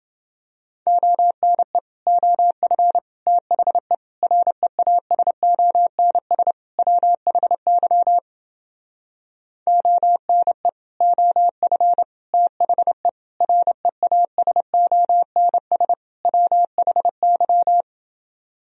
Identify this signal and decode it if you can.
Morse code, heard as a keyed tone, about 31 wpm, 700 Hz. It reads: ONE OF THE REASONS WHY ONE OF THE REASONS WHY